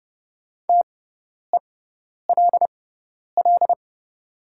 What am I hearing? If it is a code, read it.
Morse code, heard as a keyed tone, about 30 wpm, spaced as for 10 wpm, 700 Hz. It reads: TELL
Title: 6x Character Spacing - Top 200 Words - 30wpm